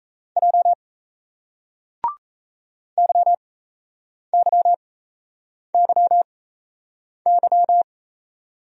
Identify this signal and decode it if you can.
Morse code, heard as a keyed tone, about 38 wpm, spaced as for 8 wpm, 700 Hz. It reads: J YYYY